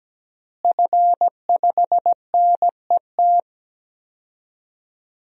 Morse code, heard as a keyed tone, about 17 wpm, 700 Hz. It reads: F5NET
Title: DXCC-100 Call Signs - 5 Characters - Repeated 3x - 17wpm